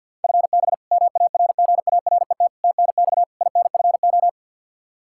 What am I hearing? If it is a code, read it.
Morse code, heard as a keyed tone, about 51 wpm, 700 Hz. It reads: FB MARGARET TNX INFO
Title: Practice QSO - Exchange 3 - More Info - 50wpm